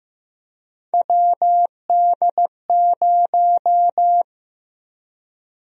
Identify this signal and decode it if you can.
Morse code, heard as a keyed tone, about 15 wpm, 700 Hz. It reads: WD0
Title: USA Call Sign Prefixes - Repeated 2x - 15wpm